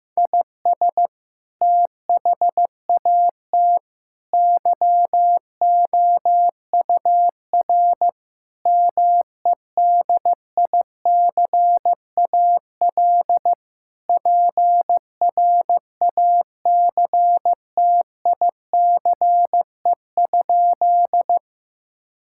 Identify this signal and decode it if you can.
Morse code, heard as a keyed tone, about 15 wpm, 700 Hz. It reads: IS THAT YOUR MEDICAL PRACTICE?
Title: Sentences from Top 1000 Words - Review Part 5 - 15wpm